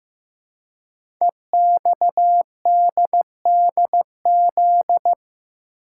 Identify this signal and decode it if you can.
Morse code, heard as a keyed tone, about 15 wpm, 700 Hz. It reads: EXDDZ
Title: Code Groups - 5 Letters - 15wpm